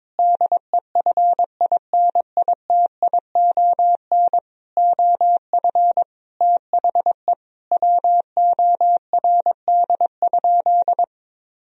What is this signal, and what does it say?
Morse code, heard as a keyed tone, about 21 wpm, 700 Hz. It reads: DEFINITION OF THE WORD?
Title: Sentences from Top 1000 Words - Review Part 4 - 22wpm